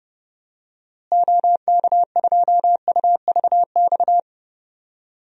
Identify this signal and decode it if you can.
Morse code, heard as a keyed tone, about 32 wpm, 700 Hz. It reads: OK2UVX
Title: DXCC-100 Call Signs - 6 Characters - Repeated 3x - 30wpm